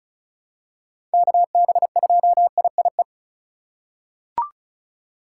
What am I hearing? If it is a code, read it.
Morse code, heard as a keyed tone, about 34 wpm, 700 Hz. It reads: KB2IIE
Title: USA Call Signs - 6 Characters - 4x Character Spacing - 35wpm